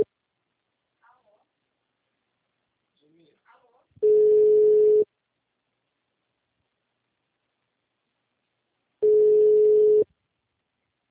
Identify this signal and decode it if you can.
Ringback tone (Europe, 425 Hz), with speech.
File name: out- -2025- - - .2392